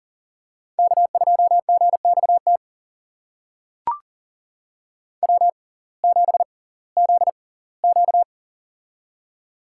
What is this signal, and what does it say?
Morse code, heard as a keyed tone, about 40 wpm, spaced as for 22 wpm, 700 Hz. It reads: K2GXT W7ZQ
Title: USA Call Signs - 6x Character Spacing - 40wpm